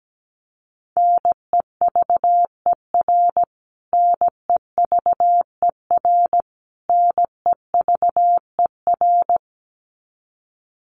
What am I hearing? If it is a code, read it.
Morse code, heard as a keyed tone, about 17 wpm, 700 Hz. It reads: NEVER NEVER NEVER
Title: Top 300 Words Repeated 17wpm